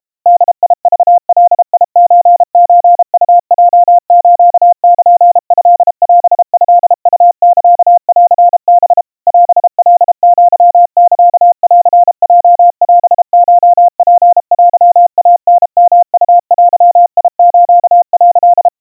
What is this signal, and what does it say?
Morse code (Japanese Wabun code), heard as a keyed tone, about 32 wpm, 700 Hz. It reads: ボクガソソウヲスルトオトウサンハ オカアサンヲオコツテイタヨウデス」
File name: kawa / Morse sample / kamyu 120 cpm 700 Hz Crispy